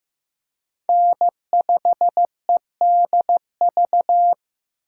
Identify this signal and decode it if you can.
Morse code, heard as a keyed tone, about 15 wpm, 700 Hz. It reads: N5EDV
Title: USA Callsign DE Callsign 15wpm